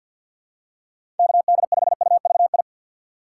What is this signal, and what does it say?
Morse code, heard as a keyed tone, about 52 wpm, 700 Hz. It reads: KD5UVI
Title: USA Call Signs - 6 Characters - 6x Character Spacing - 50wpm